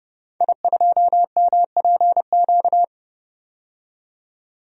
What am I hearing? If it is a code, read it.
Morse code, heard as a keyed tone, about 30 wpm, 700 Hz. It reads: I2MPQ
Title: DXCC-100 Call Signs - 5 Characters - 30wpm